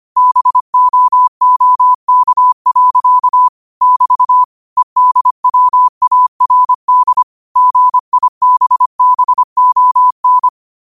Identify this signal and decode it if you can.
Morse code, heard as a keyed tone, about 25 wpm, 1000 Hz. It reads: BOOK. <BT> EDWARD GIBBON